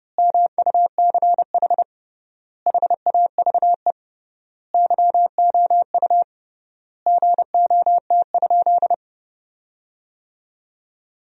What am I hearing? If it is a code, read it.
Morse code, heard as a keyed tone, about 30 wpm, 700 Hz. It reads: MUCH HAVE YOU GOT?
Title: 3x Word-Spacing - Sentences from Top 300 Words - 30wpm